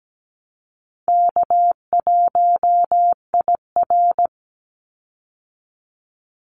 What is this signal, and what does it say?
Morse code, heard as a keyed tone, about 17 wpm, 700 Hz. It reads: K1IR
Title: Call Signs - 17wpm